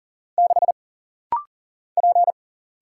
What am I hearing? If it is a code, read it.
Morse code, heard as a keyed tone, about 40 wpm, 700 Hz. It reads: 6 P
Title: Single Letter-Number Alt - Rapid-Fire 40wpm